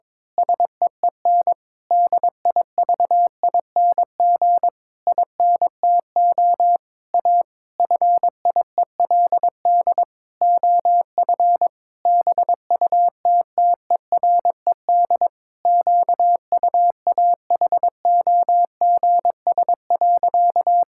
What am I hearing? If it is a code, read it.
Morse code, heard as a keyed tone, about 22 wpm, 700 Hz. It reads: SEEN DIVING INTO A FIELD OF BUTTERED QUAHOGS.
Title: Pangram Challenge v2 - 22wpm